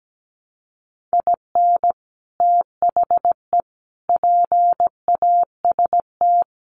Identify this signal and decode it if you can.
Morse code, heard as a keyed tone, about 17 wpm, 700 Hz. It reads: IN THE PAST
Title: Sets of 5 Words 17wpm